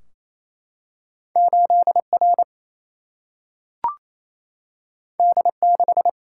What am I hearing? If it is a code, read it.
Morse code, heard as a keyed tone, about 28 wpm, 700 Hz. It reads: 8R D6
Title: Single-Letter Single-Number 28wpm